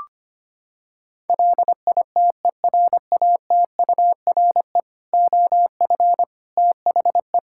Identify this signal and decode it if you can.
Morse code, heard as a keyed tone, about 25 wpm, 700 Hz. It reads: LITERATURE OF THE
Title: Sentences from Top 1000 Words - Review Part 5 - 25wpm